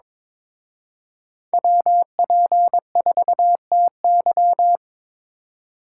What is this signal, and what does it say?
Morse code, heard as a keyed tone, about 26 wpm, 700 Hz. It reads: WP4TY
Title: Speed Racing - USA 2x2 Call Signs 22wpm